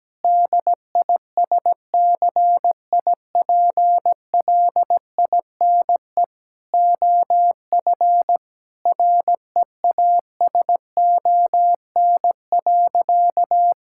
Morse code, heard as a keyed tone, about 17 wpm, 700 Hz. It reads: DISCIPLINE OF REASON.
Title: Sentences from Top 1000 Words - Review Part 6 - 17wpm